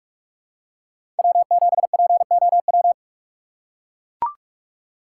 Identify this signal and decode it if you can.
Morse code, heard as a keyed tone, about 42 wpm, 700 Hz. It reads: W7POW
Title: USA Call Signs - 5 Characters - 3x Character Spacing - 45wpm